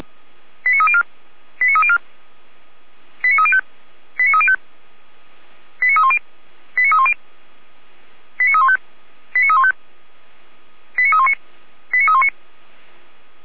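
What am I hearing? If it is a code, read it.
Five-tone selective call (ZVEI), heard as quick runs of five tones, tones of about 70 ms. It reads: 89304 89304 89305 89305 89310 89310 89315 89315 89319 89319
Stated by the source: ZVEI five-tone call received on an SDR, from the Feuerwehr Orlamünde fire brigade.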